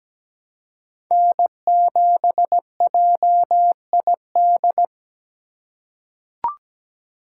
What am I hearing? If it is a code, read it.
Morse code, heard as a keyed tone, about 17 wpm, 700 Hz. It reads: N7JID
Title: USA Call Signs - 4x Character Spacing - 17wpm